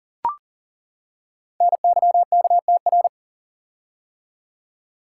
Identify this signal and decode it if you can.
Morse code, heard as a keyed tone, about 39 wpm, 700 Hz. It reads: NYKTR
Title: Code Groups - 5 Letters - 40wpm